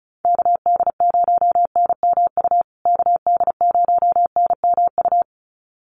Morse code, heard as a keyed tone, about 34 wpm, 700 Hz. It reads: KD0NMU KD0NMU
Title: USA Call Signs Repeated 2x 35wpm